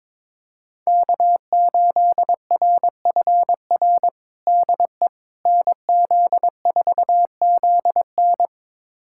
Morse code, heard as a keyed tone, about 22 wpm, 700 Hz. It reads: K8RFR DE NZ4ZN
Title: USA Callsign DE Callsign - 5 Characters - 22wpm